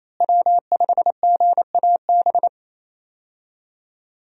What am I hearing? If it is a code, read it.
Morse code, heard as a keyed tone, about 27 wpm, 700 Hz. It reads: W5GAB
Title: USA Call Signs - 5 Characters - Repeated 2x - 28wpm